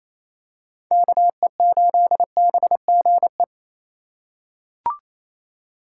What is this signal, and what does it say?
Morse code, heard as a keyed tone, about 28 wpm, 700 Hz. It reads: KE8BGE
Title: USA Call Signs - 6 Characters - 2x Character Spacing - 28wpm